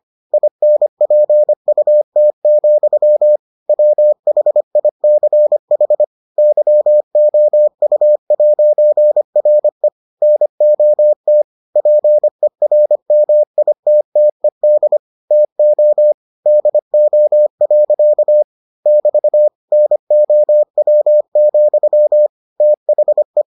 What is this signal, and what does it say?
Morse code, heard as a keyed tone, about 25 wpm, 600 Hz. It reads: INPUT, WHICH YOU'RE NOT PERMITTED TO DO. <BT> NOW, THE